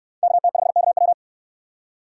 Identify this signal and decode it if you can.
Morse code, heard as a keyed tone, about 64 wpm, 700 Hz. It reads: KT5KW